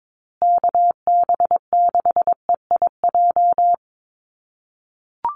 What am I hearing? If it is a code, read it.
Morse code, heard as a keyed tone, about 22 wpm, 700 Hz. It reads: KB6EIJ
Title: USA Call Signs Repeated 2x 22wpm